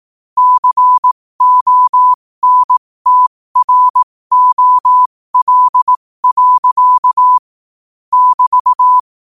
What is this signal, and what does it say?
Morse code, heard as a keyed tone, about 18 wpm, spaced as for 15 wpm, 1000 Hz. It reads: CONTROL. <BT>